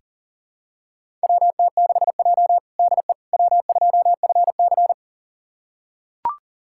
Morse code, heard as a keyed tone, about 40 wpm, 700 Hz. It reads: WT6J DE W2FC
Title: USA Callsign DE Callsign - 4 Characters - 40wpm